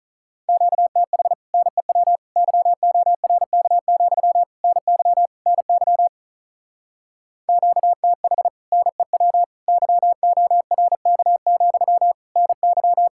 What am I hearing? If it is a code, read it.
Morse code, heard as a keyed tone, about 45 wpm, 700 Hz. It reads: QTH NEW YORK, NY NY QTH NEW YORK, NY